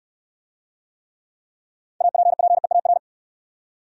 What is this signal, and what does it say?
Morse code, heard as a keyed tone, about 64 wpm, 700 Hz. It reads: APPEAR